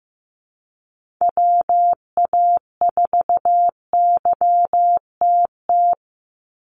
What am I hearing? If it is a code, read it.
Morse code, heard as a keyed tone, about 15 wpm, 700 Hz. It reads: WA4YTT
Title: USA Call Signs 15wpm